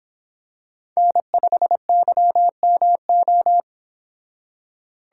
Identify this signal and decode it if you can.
Morse code, heard as a keyed tone, about 25 wpm, 700 Hz. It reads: N5YMO